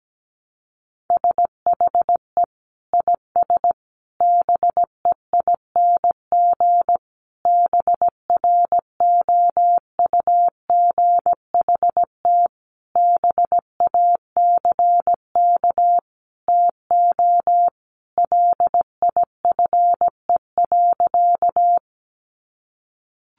Morse code, heard as a keyed tone, about 17 wpm, 700 Hz. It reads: SHE IS BEING BROUGHT BACK TO LIFE.
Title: Sentences from Top 600 Words 17wpm